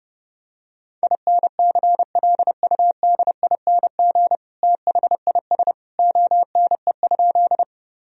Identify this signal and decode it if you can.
Morse code, heard as a keyed tone, about 30 wpm, 700 Hz. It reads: INCLUDING THIS ONE?
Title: Sentences from Top 1000 Words - Review Part 4 - 30wpm